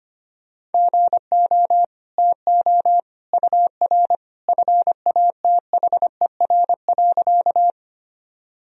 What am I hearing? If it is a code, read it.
Morse code, heard as a keyed tone, about 25 wpm, 700 Hz. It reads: GO TO UR FATHER.
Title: Sentences from Top 300 Words and Abbreviations 25wpm